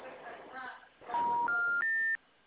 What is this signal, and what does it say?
Special information tone, with speech.